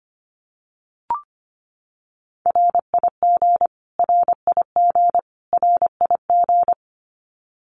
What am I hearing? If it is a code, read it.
Morse code, heard as a keyed tone, about 25 wpm, 700 Hz. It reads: RIG RIG RIG
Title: Abbreviations Repeated 3x - 25wpm